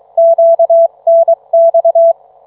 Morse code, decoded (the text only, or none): QNX